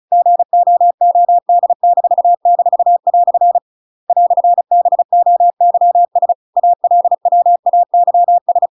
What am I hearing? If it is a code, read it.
Morse code, heard as a keyed tone, about 34 wpm, 700 Hz. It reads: GOOD--" "BOYS ALWAYS
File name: Pinocchio / Pinocchio ch8 35 35 ik1hge